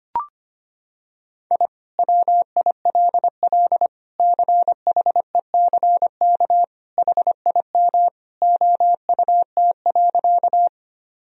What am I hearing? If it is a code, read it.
Morse code, heard as a keyed tone, about 25 wpm, 700 Hz. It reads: I WILL CHECK HIM OUT.
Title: Sentences from Top 1000 Words - Review Part 3 - 25wpm